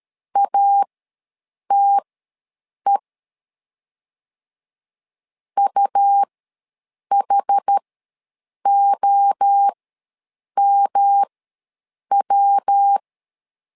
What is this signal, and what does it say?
Morse code, heard as a keyed tone, about 13 wpm, spaced as for 7 wpm, 800 Hz. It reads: ATE UHOMW
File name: LEZIONI / 3 MWRO / 3 MWRO ws 12wpm 1914